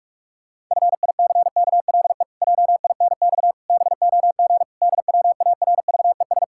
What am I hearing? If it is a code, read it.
Morse code, heard as a keyed tone, about 46 wpm, 700 Hz. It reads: FICKLE JINX BOG DWARVES
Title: Pangram Challenge v2 - 45wpm